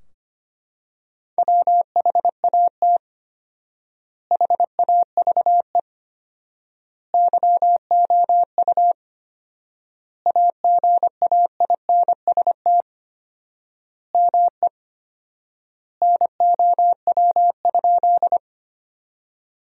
Morse code, heard as a keyed tone, about 25 wpm, 700 Hz. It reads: WHAT HAVE YOU AGAINST ME NOW?
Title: 4x Word-Spacing - Sentences from Top 400 Words - 25wpm